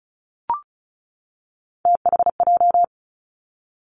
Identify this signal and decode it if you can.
Morse code, heard as a keyed tone, about 36 wpm, 700 Hz. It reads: THJ
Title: Three Challenging Letter-Number Combinations 35wpm